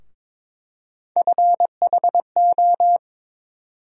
Morse code, heard as a keyed tone, about 22 wpm, 700 Hz. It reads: FHO
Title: USA Call Sign Postfixes - 3 Letters 22wpm